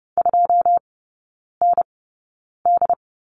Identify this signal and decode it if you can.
Morse code, heard as a keyed tone, about 30 wpm, spaced as for 9 wpm, 700 Hz. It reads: 2ND
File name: course / 30:10wpm / 087 Lesson 15e - Number 5 - Three Characters 30wpm